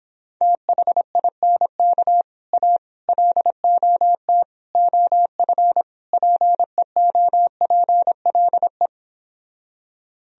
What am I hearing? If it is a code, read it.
Morse code, heard as a keyed tone, about 31 wpm, 700 Hz. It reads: THINK A LOT OF PEOPLE